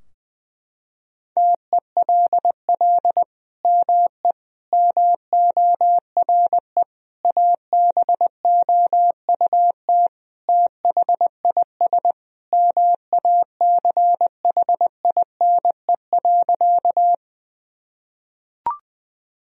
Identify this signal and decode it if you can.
Morse code, heard as a keyed tone, about 21 wpm, 700 Hz. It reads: TELL ME MORE ABOUT THIS MACHINE.
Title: Sentences from Top 1000 Words - Review Part 2 - 20wpm